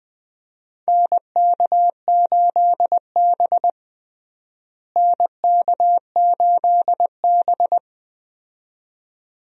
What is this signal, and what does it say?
Morse code, heard as a keyed tone, about 20 wpm, 700 Hz. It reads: NK8B NK8B
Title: USA Call Signs - 4 Characters - Repeated 2x - 20wpm